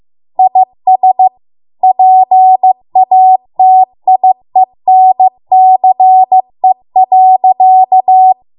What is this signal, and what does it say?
Morse code, heard as a keyed tone, about 15 wpm, 750 Hz. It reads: IS PATIENCE.